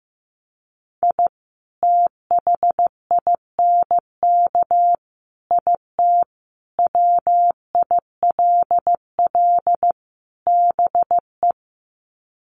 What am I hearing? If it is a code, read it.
Morse code, heard as a keyed tone, about 15 wpm, 700 Hz. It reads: I THINK IT WILL BE
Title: Sets of 5 Words Encore2 15wpm